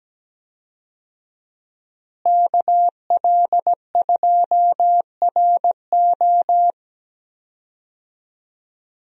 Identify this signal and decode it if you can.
Morse code, heard as a keyed tone, about 17 wpm, 700 Hz. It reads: KL2RO
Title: USA Call Signs - 5 Characters - Repeated 3x - 17wpm